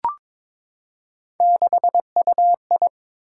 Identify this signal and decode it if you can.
Morse code, heard as a keyed tone, about 22 wpm, 700 Hz. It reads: 6UI
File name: practice / 22wpm / Three Letter-Number Combinations - 22wpm